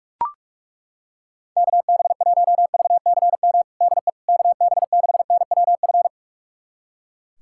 Morse code, heard as a keyed tone, about 42 wpm, 700 Hz. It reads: KB1VCM DE KB6NWF
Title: USA Callsign DE Callsign - 6 Characters - 45wpm